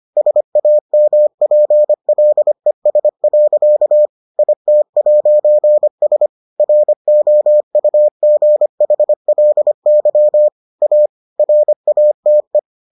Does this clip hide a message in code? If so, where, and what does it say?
Yes: Morse code from 0.2 to 12.6 s, heard as a keyed tone, about 25 wpm, 600 Hz. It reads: SAMPLES. IT'S ROUGHLY A RATE